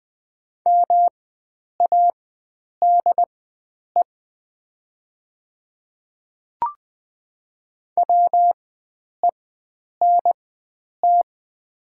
Morse code, heard as a keyed tone, about 20 wpm, spaced as for 9 wpm, 700 Hz. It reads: MADE WENT
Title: 4x Character Spacing - Top 300 Words - 20wpm